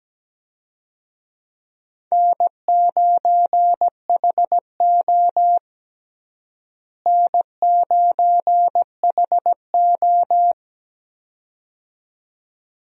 Morse code, heard as a keyed tone, about 17 wpm, 700 Hz. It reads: N9HO N9HO
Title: USA Call Signs - 4 Characters - Repeated 2x - 17wpm